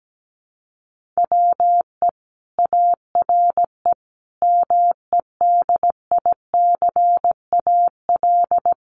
Morse code, heard as a keyed tone, about 17 wpm, 700 Hz. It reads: WE ARE MEDICAL
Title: Sentences from Top 800 Words 17wpm